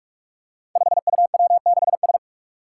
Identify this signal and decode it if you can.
Morse code, heard as a keyed tone, about 46 wpm, 700 Hz. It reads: 5UW6S
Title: Five Challenging Letter-Number Combinations 45wpm